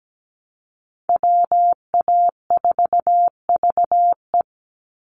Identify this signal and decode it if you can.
Morse code, heard as a keyed tone, about 17 wpm, 700 Hz. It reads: WA4VE
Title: USA Call Signs 17wpm